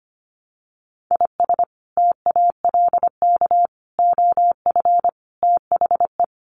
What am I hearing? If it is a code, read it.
Morse code, heard as a keyed tone, about 25 wpm, 700 Hz. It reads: IS TALK OF THE
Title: Sets of 5 Words 25wpm